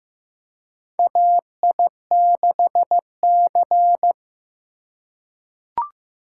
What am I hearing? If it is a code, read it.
Morse code, heard as a keyed tone, about 15 wpm, 700 Hz. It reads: AI6C
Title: USA Call Signs - 4x Character Spacing - 15wpm